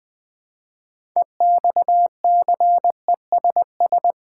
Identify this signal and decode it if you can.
Morse code, heard as a keyed tone, about 20 wpm, 700 Hz. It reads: EXCESS
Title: 6 Letter Words - First 3 Letters Given - 20wpm